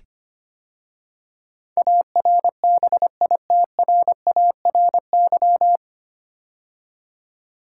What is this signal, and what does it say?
Morse code, heard as a keyed tone, about 25 wpm, 690 Hz. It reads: ARBITRARY
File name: practice / 25wpm / 9 Letter Words - First 3 Letters Given - 25wpm